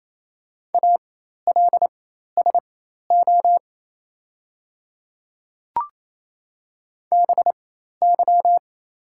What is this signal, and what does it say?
Morse code, heard as a keyed tone, about 27 wpm, spaced as for 13 wpm, 700 Hz. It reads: ALSO BY
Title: 4x Character Spacing - Top 300 Words - 28wpm